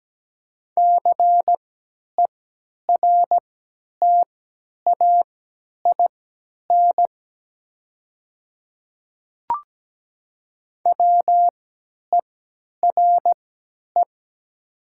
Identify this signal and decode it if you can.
Morse code, heard as a keyed tone, about 17 wpm, spaced as for 10 wpm, 690 Hz. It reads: CERTAIN WERE
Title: 3x Character Spacing - Top 500 Words - 17wpm